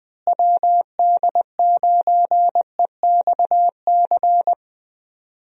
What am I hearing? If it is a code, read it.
Morse code, heard as a keyed tone, about 21 wpm, 700 Hz. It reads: WD9EXC
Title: USA Call Signs - 6 Characters - Repeated 3x - 20wpm